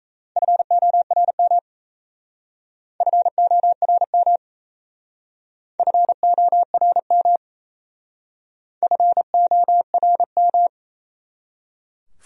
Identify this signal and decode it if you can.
Morse code, heard as a keyed tone, about 37 wpm, 700 Hz. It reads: FORM FORM FORM FORM